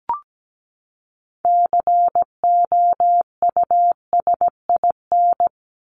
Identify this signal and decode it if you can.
Morse code, heard as a keyed tone, about 17 wpm, 700 Hz. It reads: COUSIN
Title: Top 1501-1750 Words 17wpm